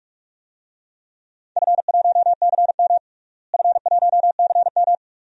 Morse code, heard as a keyed tone, about 42 wpm, 700 Hz. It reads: F1CM F1CM
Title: DXCC-100 Call Signs - 4 Characters - Repeated 2x - 45wpm